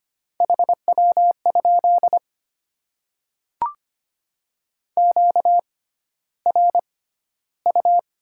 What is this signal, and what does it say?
Morse code, heard as a keyed tone, about 25 wpm, spaced as for 13 wpm, 690 Hz. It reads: HW? QRU
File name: Top 90 QSO Elements - 6x Character Spacing - 25wpm